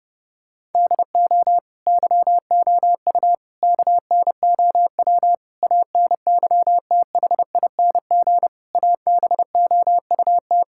Morse code, heard as a keyed tone, about 32 wpm, 700 Hz. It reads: DO YOU KNOW ANYTHING ABOUT
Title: Sentences from Top 1600 Words 30wpm